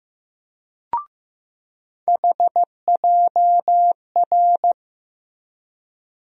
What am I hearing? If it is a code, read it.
Morse code, heard as a keyed tone, about 15 wpm, 700 Hz. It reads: HJR
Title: USA Call Sign Postfixes - 3 Letters 15wpm